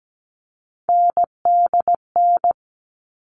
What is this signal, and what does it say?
Morse code, heard as a keyed tone, about 17 wpm, 700 Hz. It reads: NDN
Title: All Cut Numbers - 3-Digit 17wpm